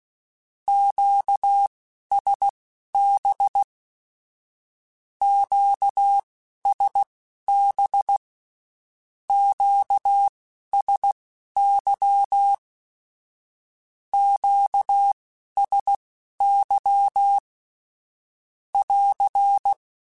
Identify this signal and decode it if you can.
Morse code, heard as a keyed tone, about 16 wpm, spaced as for 11 wpm, 800 Hz. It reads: QSB QSB QSY QSY <AR>
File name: (spaced) / Q-Calls Spaced-mp3s / Q Calls Twice sp @15